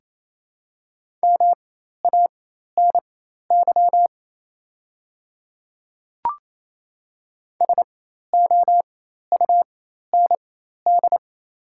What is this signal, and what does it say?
Morse code, heard as a keyed tone, about 28 wpm, spaced as for 13 wpm, 700 Hz. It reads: MANY SOUND
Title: 4x Character Spacing - Top 100 Words - 28wpm